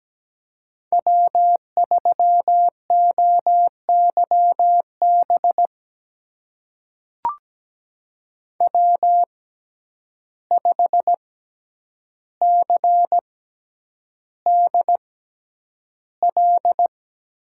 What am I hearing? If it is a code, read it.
Morse code, heard as a keyed tone, about 17 wpm, spaced as for 9 wpm, 700 Hz. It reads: W3OYB W5CDL
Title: USA Call Signs - 5 Characters - 6x Character Spacing - 17wpm